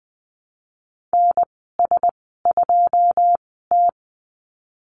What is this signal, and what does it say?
Morse code, heard as a keyed tone, about 20 wpm, spaced as for 15 wpm, 700 Hz. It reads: NS2T
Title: 154 Lesson 25g - Letter p - Call Signs 20wpm